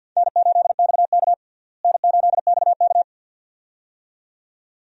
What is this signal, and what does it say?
Morse code, heard as a keyed tone, about 52 wpm, 700 Hz. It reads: N8XK N8XK
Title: USA Call Signs - 4 Characters - Repeated 2x - 50wpm